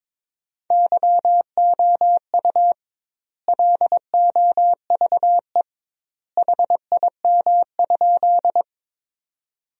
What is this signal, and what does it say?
Morse code, heard as a keyed tone, about 22 wpm, 700 Hz. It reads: YOU LOVE HIM?
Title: 2x Word-Spacing - Sentences from Top 400 Words - 22wpm